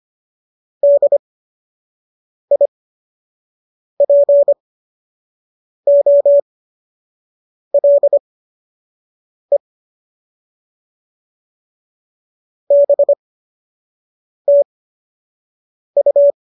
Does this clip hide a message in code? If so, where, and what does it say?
Yes: Morse code from 0.8 to 16.3 s, heard as a keyed tone, about 25 wpm, spaced as for 6 wpm, 570 Hz. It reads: DIPOLE BTU